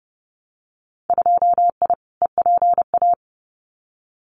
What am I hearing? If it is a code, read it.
Morse code, heard as a keyed tone, about 30 wpm, 700 Hz. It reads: 2I EPA
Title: Winter Field Day Exchanges 30wpm